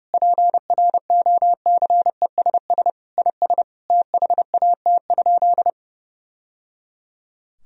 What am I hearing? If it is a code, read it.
Morse code, heard as a keyed tone, about 30 wpm, 700 Hz. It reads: PROCESS IS THAT?
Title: Sentences from Top 1000 Words - Review Part 4 - 30wpm